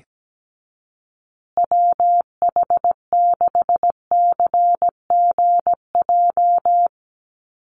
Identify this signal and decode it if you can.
Morse code, heard as a keyed tone, about 17 wpm, 700 Hz. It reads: WH6CGJ
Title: USA Call Signs Repeated 3x 17wpm